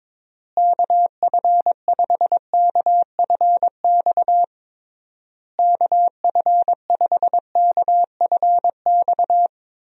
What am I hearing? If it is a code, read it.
Morse code, heard as a keyed tone, about 21 wpm, 700 Hz. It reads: KF5KFX KF5KFX
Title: USA Call Signs - 6 Characters - Repeated 3x - 22wpm